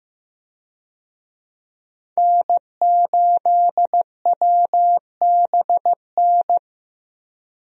Morse code, heard as a keyed tone, about 15 wpm, 700 Hz. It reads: N8WBN
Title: USA Call Signs - 5 Characters - Repeated 2x - 15wpm